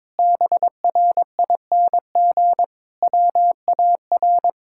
Morse code, heard as a keyed tone, about 22 wpm, 700 Hz. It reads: BRING WAR
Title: Sentences from Top 1000 Words - Review Part 6 - 22wpm